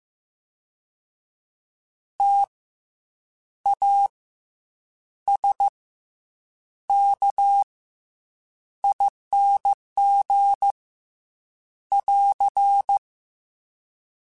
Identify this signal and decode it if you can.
Morse code, heard as a keyed tone, about 15 wpm, 800 Hz. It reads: T A S K ING <AR>
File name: Special INGs / Special INGs-mp3s / SlowToFaster ing @13